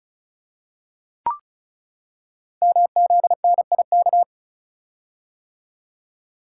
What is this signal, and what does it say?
Morse code, heard as a keyed tone, about 36 wpm, 690 Hz. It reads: MZNIK
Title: Code Groups - 5 Letters - 35wpm